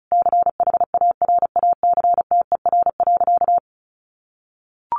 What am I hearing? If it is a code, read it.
Morse code, heard as a keyed tone, about 34 wpm, 700 Hz. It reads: CHARACTER.
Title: Sentences from Top 1200 Words 35wpm